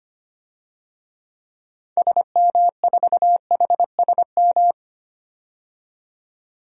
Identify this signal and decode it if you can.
Morse code, heard as a keyed tone, about 25 wpm, 700 Hz. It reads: SM4HSM
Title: DXCC-100 Call Signs - 6 Characters - 25wpm